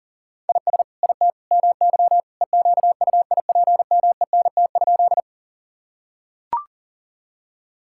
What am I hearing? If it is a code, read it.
Morse code, heard as a keyed tone, about 40 wpm, 700 Hz. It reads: IS IT MY EQUIPMENT?
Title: Sentences from Top 1000 Words - Review Part 5 - 40wpm